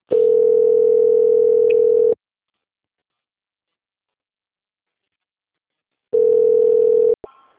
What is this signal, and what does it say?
Ringback tone (North America), 440 + 480 Hz.